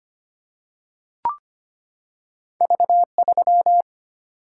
Morse code, heard as a keyed tone, about 25 wpm, 700 Hz. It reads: V3
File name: Two Letter-Number Combinations - 25wpm